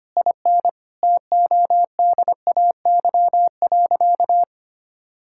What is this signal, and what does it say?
Morse code, heard as a keyed tone, about 25 wpm, 700 Hz. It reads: IN TODAY.
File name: practice / 25wpm / Sentences with 4-Word Phrases - 25wpm